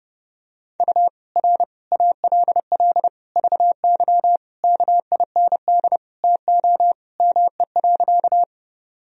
Morse code, heard as a keyed tone, about 32 wpm, 700 Hz. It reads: U R ALL VY KIND TO ME.
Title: Sentences from Top 200 Words and Abbreviations 30wpm